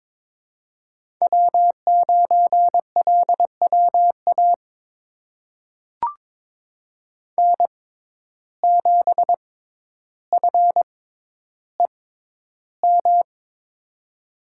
Morse code, heard as a keyed tone, about 22 wpm, spaced as for 11 wpm, 700 Hz. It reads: W9LWA N7FEM
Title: USA Call Signs - 5 Characters - 6x Character Spacing - 22wpm